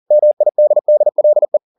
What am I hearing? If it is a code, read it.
Morse code, heard as a keyed tone, about 40 wpm, 600 Hz. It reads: MIDDLE
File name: the old man and the sea 2 40wpm